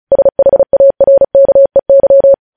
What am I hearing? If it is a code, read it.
Morse code, heard as a keyed tone, about 36 wpm, 550 Hz. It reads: SHARKEY
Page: news headlines in morse code at 35 wpm